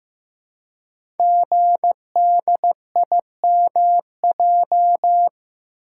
Morse code, heard as a keyed tone, about 15 wpm, 700 Hz. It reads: GDIMJ
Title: Code Groups - 5 Letters - 15wpm